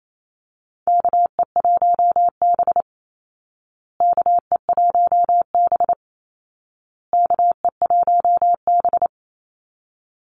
Morse code, heard as a keyed tone, about 28 wpm, 700 Hz. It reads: KE1B KE1B KE1B
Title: Call Signs Repeated 2x - 28wpm